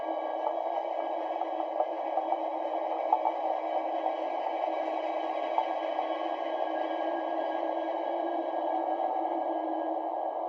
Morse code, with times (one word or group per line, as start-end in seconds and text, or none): none